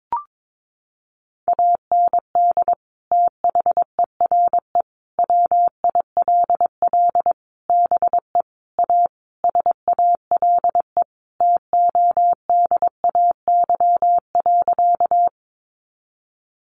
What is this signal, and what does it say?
Morse code, heard as a keyed tone, about 22 wpm, 700 Hz. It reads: AND THERE WILL BE A SALE TODAY.
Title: Sentences from Top 1400 Words 22wpm